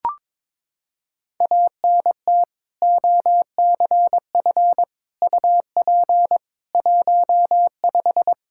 Morse code, heard as a keyed tone, about 21 wpm, 700 Hz. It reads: ANT OCF UP 15
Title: Practice QSO - Exchange 2 - Info - 22wpm